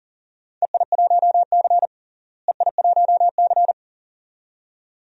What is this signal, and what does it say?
Morse code, heard as a keyed tone, about 37 wpm, 700 Hz. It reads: EI1C EI1C
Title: DXCC-100 Call Signs - 4 Characters - Repeated 2x - 40wpm